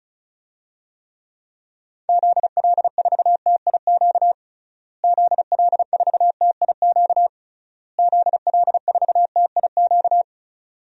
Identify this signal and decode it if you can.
Morse code, heard as a keyed tone, about 34 wpm, 700 Hz. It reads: ZL4TIQ ZL4TIQ ZL4TIQ
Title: DXCC-100 Call Signs - 6 Characters - Repeated 3x - 35wpm